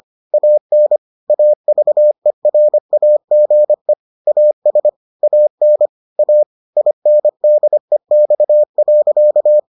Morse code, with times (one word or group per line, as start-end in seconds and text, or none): 0.3-1.0 AN
1.3-3.9 AVERAGE
4.3-4.9 AS
5.2-5.9 AN
6.2-6.4 A
6.8-9.6 INDEX.